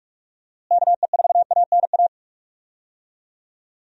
Morse code, heard as a keyed tone, about 46 wpm, 700 Hz. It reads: KE4ANA
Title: USA Call Signs - 6 Characters - Repeated 3x - 45wpm